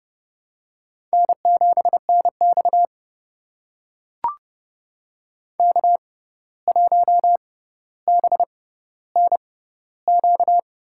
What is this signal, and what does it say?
Morse code, heard as a keyed tone, about 32 wpm, spaced as for 15 wpm, 700 Hz. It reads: N7NX K1BNQ
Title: USA Call Signs - 6x Character Spacing - 30wpm